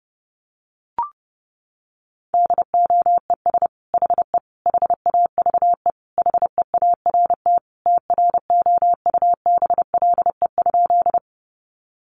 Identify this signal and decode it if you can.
Morse code, heard as a keyed tone, about 30 wpm, 700 Hz. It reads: DOES HE HAVE HEART TROUBLE?
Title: Sentences from Top 1200 Words 30wpm